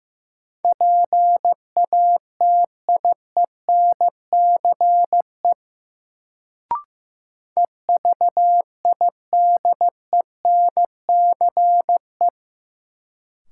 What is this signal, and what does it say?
Morse code, heard as a keyed tone, about 15 wpm, 700 Hz. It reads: PATIENCE EVIDENCE
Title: Sentences from Top 1300 Words - New Word Drill - 15wpm